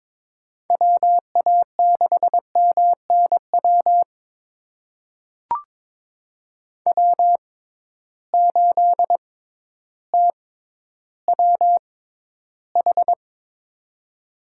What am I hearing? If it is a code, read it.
Morse code, heard as a keyed tone, about 22 wpm, spaced as for 12 wpm, 700 Hz. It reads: WA6MNW W8TWH
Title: USA Call Signs - 6x Character Spacing - 22wpm